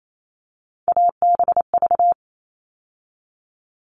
Morse code, heard as a keyed tone, about 28 wpm, 700 Hz. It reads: ABV
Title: All Cut Numbers - 3-Digit 28wpm